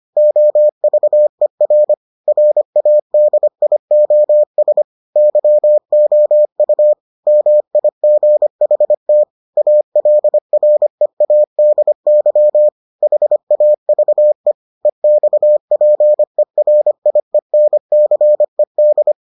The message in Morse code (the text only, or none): OVER RADIOS YOU MIGHT ALREADY HAVE EXPERIENCED